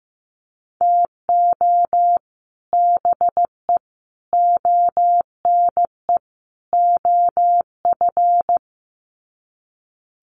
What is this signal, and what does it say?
Morse code, heard as a keyed tone, about 15 wpm, 700 Hz. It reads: TO BE ONE OF
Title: Sets of 4 Words 15wpm